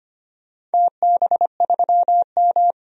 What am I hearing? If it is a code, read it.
Morse code, heard as a keyed tone, about 25 wpm, 700 Hz. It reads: TB3M
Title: DXCC-100 Call Signs - 4 Characters - 25wpm